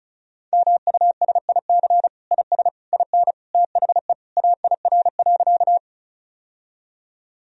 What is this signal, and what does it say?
Morse code, heard as a keyed tone, about 36 wpm, 700 Hz. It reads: MUSIC IS IN THE AIR.
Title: Sentences from Top 1000 Words - Review Part 1 - 35wpm